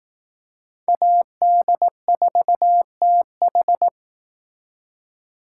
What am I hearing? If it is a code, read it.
Morse code, heard as a keyed tone, about 18 wpm, 700 Hz. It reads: AD4TH
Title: Speed Racing - USA 2x2 Call Signs 15wpm